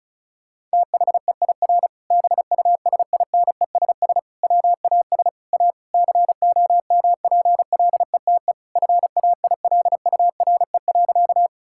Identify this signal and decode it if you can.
Morse code, heard as a keyed tone, about 34 wpm, 700 Hz. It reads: THEIR BUSINESS WAS A COMPLETE FAILURE.
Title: Sentences from Top 1000 Words - Review Part 6 - 35wpm